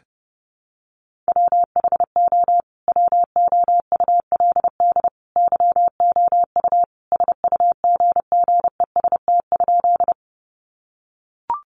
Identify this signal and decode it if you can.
Morse code, heard as a keyed tone, about 30 wpm, 700 Hz. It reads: WHO WOULD YOU SUGGEST?
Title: Sentences from Top 1300 Words 30wpm